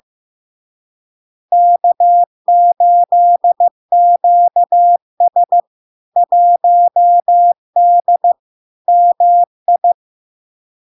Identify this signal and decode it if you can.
Morse code, heard as a keyed tone, about 15 wpm, 700 Hz. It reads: K8QS 1D MI
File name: LARK-FD-2022-Log-in-CW-15wpm-700-Hz